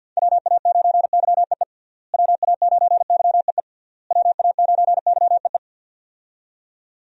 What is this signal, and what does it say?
Morse code, heard as a keyed tone, about 51 wpm, 700 Hz. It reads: WA9YEE WA9YEE WA9YEE
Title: USA Call Signs - 6 Characters - Repeated 3x - 50wpm